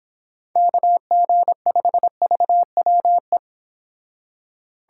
Morse code, heard as a keyed tone, about 28 wpm, 700 Hz. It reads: KG5VWE